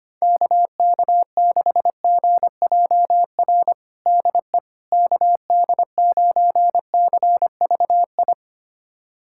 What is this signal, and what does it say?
Morse code, heard as a keyed tone, about 25 wpm, 700 Hz. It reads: KK6GJR DE KD9CVI
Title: USA Callsign DE Callsign - 6 Characters - 25wpm